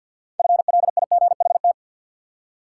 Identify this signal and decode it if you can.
Morse code, heard as a keyed tone, about 52 wpm, 700 Hz. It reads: FLIGHT
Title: Six Letter Words - 50wpm